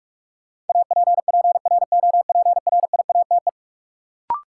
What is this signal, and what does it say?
Morse code, heard as a keyed tone, about 42 wpm, 700 Hz. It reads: APPROPRIATE
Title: Sentences from Top 900 Words - New Word Drill - 45wpm